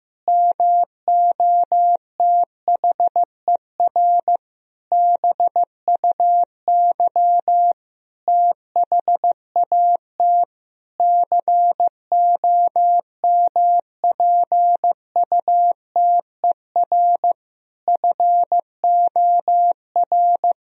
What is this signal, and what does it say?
Morse code, heard as a keyed tone, about 15 wpm, 700 Hz. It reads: MOTHER BUY THAT COMPUTER FOR
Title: Sentences from Top 1000 Words - Review Part 3 - 15wpm